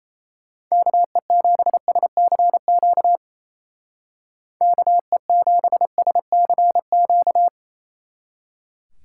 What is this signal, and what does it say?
Morse code, heard as a keyed tone, about 33 wpm, 700 Hz. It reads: KE7SCQ KE7SCQ
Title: Speed Racing - USA 2x3 Call Signs 28wpm